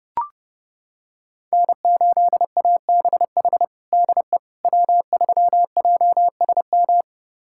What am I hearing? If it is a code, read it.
Morse code, heard as a keyed tone, about 30 wpm, 700 Hz. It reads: N8ABH DE W3JSM
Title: USA Callsign DE Callsign - 5 Characters - 30wpm